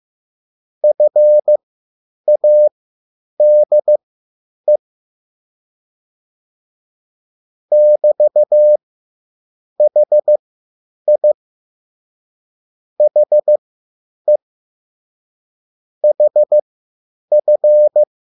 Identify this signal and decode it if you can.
Morse code, heard as a keyed tone, about 15 wpm, spaced as for 8 wpm, 600 Hz. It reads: FADE <BT>HI HE HF